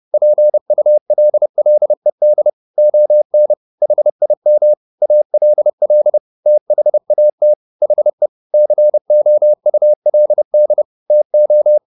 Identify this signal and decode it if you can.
Morse code, heard as a keyed tone, about 30 wpm, 600 Hz. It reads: PULLED ON HIM ALL THAT HE COULD TO